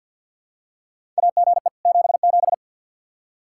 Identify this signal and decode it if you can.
Morse code, heard as a keyed tone, about 52 wpm, 700 Hz. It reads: AGE 77
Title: Practice QSO - Exchange 3 - More Info - 50wpm